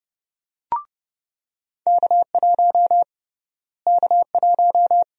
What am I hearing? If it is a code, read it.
Morse code, heard as a keyed tone, about 30 wpm, 700 Hz. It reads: K1 K1
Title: USA Call Sign Prefixes - Repeated 2x - 30wpm